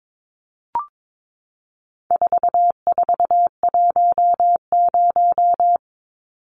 Morse code, heard as a keyed tone, about 22 wpm, 700 Hz. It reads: 4410
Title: Numbers - 4-Digit 22wpm